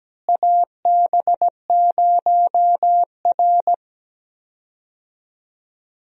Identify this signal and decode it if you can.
Morse code, heard as a keyed tone, about 17 wpm, 700 Hz. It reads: AB0R
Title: USA Call Signs - 4 Characters - Repeated 2x - 17wpm